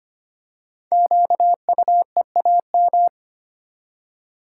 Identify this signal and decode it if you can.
Morse code, heard as a keyed tone, about 25 wpm, 700 Hz. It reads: QUEAM
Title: Code Groups - 5 Letters - 25wpm